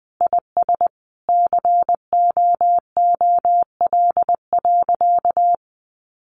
Morse code, heard as a keyed tone, about 20 wpm, 700 Hz. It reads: IS COOL.
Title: Sentences from Top 1200 Words 20wpm